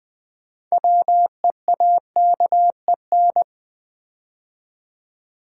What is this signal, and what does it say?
Morse code, heard as a keyed tone, about 20 wpm, 700 Hz. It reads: WEAKEN
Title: Build A Word - 3 Letters At A Time - 6-Letter Words - 20wpm